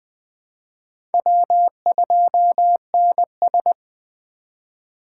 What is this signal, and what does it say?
Morse code, heard as a keyed tone, about 21 wpm, 700 Hz. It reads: W2NS
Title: USA Call Signs - 3x Character Spacing - 20wpm